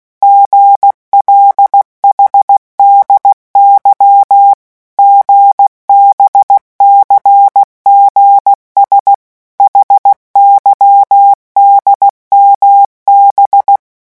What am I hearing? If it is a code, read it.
Morse code, heard as a keyed tone, about 16 wpm, 800 Hz. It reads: GLHDY GBCGS HYDMB
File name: morse mp3 / revision5-vitesse 15 mots